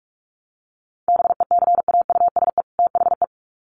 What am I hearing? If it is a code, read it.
Morse code, heard as a keyed tone, about 46 wpm, 700 Hz. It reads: BECAUSE THE
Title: Sets of 2 Words 45wpm